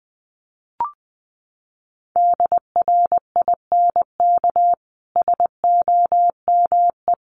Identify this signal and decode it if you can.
Morse code, heard as a keyed tone, about 20 wpm, 700 Hz. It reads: DRINK SOME
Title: Sentences from Top 1500 Words 20wpm